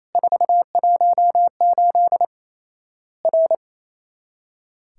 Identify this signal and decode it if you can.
Morse code, heard as a keyed tone, about 28 wpm, 700 Hz. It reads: 418 R
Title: Worked All Europe DX Contest - QTC Exchange - 28wpm